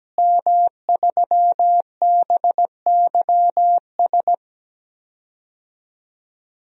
Morse code, heard as a keyed tone, about 17 wpm, 700 Hz. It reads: M3BYS
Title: DXCC-100 Call Signs - 5 Characters - Repeated 2x - 17wpm